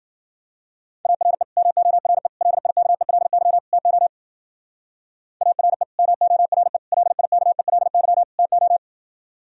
Morse code, heard as a keyed tone, about 49 wpm, 700 Hz. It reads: ARE MORE LIKELY TO ARE MORE LIKELY TO
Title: Speed-Racing - Sets of 4 Words 40wpm